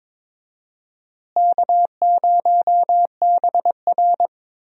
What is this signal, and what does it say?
Morse code, heard as a keyed tone, about 21 wpm, 700 Hz. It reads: K0BR
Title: USA Call Signs - 4 Characters - Repeated 2x - 22wpm